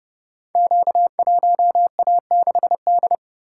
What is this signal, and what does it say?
Morse code, heard as a keyed tone, about 30 wpm, 700 Hz. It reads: Q1A6D
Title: Code Groups - 5 Letter:Numbers - 30wpm